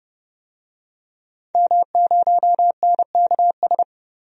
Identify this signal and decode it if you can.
Morse code, heard as a keyed tone, about 31 wpm, 700 Hz. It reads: M0NKS
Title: DXCC-100 Call Signs - 5 Characters - 30wpm